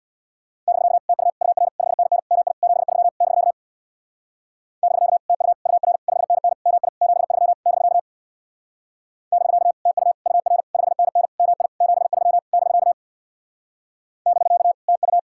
Morse code, heard as a keyed tone, about 62 wpm, 700 Hz. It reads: <BK> TU UR 5NN MI 73 <BK> <BK> TU UR 5NN MI 73 <BK> <BK> TU UR 5NN MI 73 <BK> <BK> TU
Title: Speed-Racing Common QSO Phrases - 50wpm